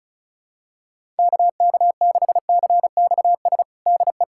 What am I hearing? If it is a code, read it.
Morse code, heard as a keyed tone, about 36 wpm, 700 Hz. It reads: KK6CXS DE
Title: USA Callsign DE Callsign - 6 Characters - 35wpm